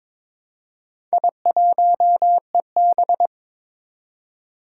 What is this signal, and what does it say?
Morse code, heard as a keyed tone, about 22 wpm, 700 Hz. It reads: I1EB
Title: Four Challenging Letter-Number Combinations 22wpm